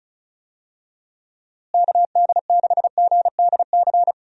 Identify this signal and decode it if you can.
Morse code, heard as a keyed tone, about 36 wpm, 700 Hz. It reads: KD6GDC
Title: USA Call Signs - 6 Characters - Repeated 2x - 35wpm